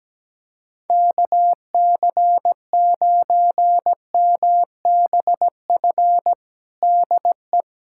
Morse code, heard as a keyed tone, about 17 wpm, 700 Hz. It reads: KC9MBF DE
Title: USA Callsign DE Callsign - 6 Characters - 17wpm